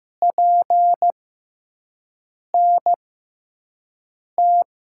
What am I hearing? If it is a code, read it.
Morse code, heard as a keyed tone, about 15 wpm, spaced as for 5 wpm, 700 Hz. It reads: PNT
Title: USA Call Signs - 6 Characters - 6x Character Spacing - 15wpm